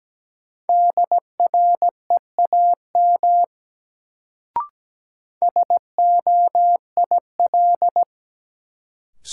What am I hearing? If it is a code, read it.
Morse code, heard as a keyed tone, about 17 wpm, 700 Hz. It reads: DREAM SOIL
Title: Sentences from Top 1500 Words - New Word Drill Alt - 17wpm